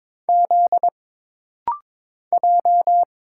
Morse code, heard as a keyed tone, about 22 wpm, 700 Hz. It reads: Z J
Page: This rapid-fire timing is geared to push your ICR (Instant Character Recognition) skills.